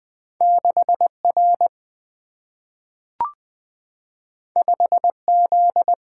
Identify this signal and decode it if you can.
Morse code, heard as a keyed tone, about 20 wpm, 700 Hz. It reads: 6R 5Z
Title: Single-Letter Single-Number 20wpm